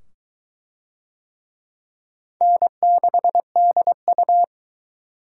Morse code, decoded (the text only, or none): N6DU